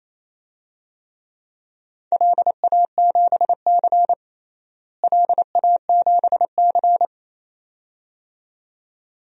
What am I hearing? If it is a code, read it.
Morse code, heard as a keyed tone, about 28 wpm, 700 Hz. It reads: LA7C LA7C
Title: DXCC-100 Call Signs - 4 Characters - Repeated 2x - 28wpm